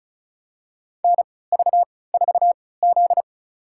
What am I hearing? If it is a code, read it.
Morse code, heard as a keyed tone, about 34 wpm, spaced as for 20 wpm, 690 Hz. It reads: NV4Z
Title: USA Call Signs - 3x Character Spacing - 35wpm